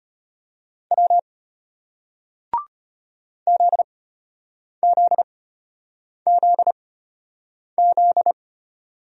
Morse code, heard as a keyed tone, about 36 wpm, spaced as for 8 wpm, 700 Hz. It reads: W ZZZZ